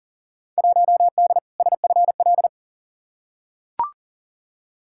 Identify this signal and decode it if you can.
Morse code, heard as a keyed tone, about 40 wpm, 700 Hz. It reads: 1D SFL
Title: Field Day Exchanges - 40wpm